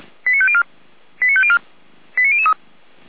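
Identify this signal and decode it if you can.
Five-tone selective call (ZVEI), heard as quick runs of five tones, tones of about 70 ms. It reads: 89504 89504 89003
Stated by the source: Five-tone ZVEI selective call 89265 received by SDR.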